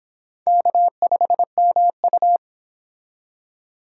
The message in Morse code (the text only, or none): K5MU